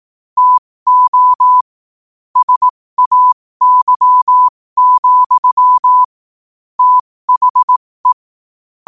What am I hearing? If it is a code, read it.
Morse code, heard as a keyed tone, about 18 wpm, spaced as for 15 wpm, 1000 Hz. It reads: TO SAY, THE
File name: qotdfiles / 15wpm